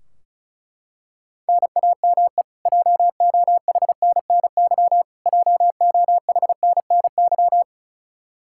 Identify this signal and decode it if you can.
Morse code, heard as a keyed tone, about 36 wpm, 700 Hz. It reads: NAME JOHNNY JOHNNY
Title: Practice QSO - Exchange 1 - Beginning - 35wpm